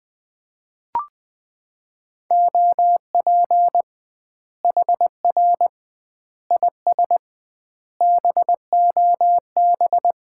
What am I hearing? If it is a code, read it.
Morse code, heard as a keyed tone, about 20 wpm, 700 Hz. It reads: OP HR IS BOB
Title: Common QSO Phrases - 2x Word-Spacing - 20wpm